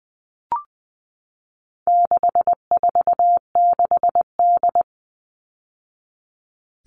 Morse code, heard as a keyed tone, about 20 wpm, 700 Hz. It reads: 646D
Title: All Cut Numbers - 4-Digit 20wpm